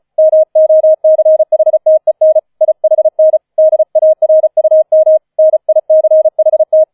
Morse code, heard as a keyed tone, about 34 wpm, 630 Hz. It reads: MOCHTEN IHN DARUM NICHT
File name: 004 Vierter Streich 35wpm